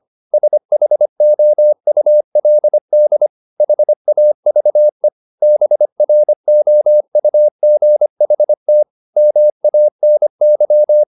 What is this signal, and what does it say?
Morse code, heard as a keyed tone, about 25 wpm, 600 Hz. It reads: SHOULD HAVE BROUGHT MANY